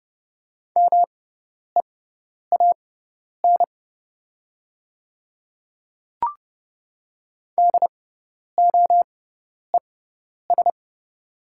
Morse code, heard as a keyed tone, about 30 wpm, spaced as for 10 wpm, 700 Hz. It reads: MEAN DOES
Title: 6x Character Spacing - Top 200 Words - 30wpm